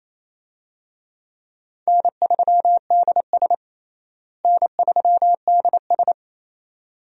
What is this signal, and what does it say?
Morse code, heard as a keyed tone, about 27 wpm, 700 Hz. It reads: N3DS N3DS
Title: USA Call Signs - 4 Characters - Repeated 2x - 28wpm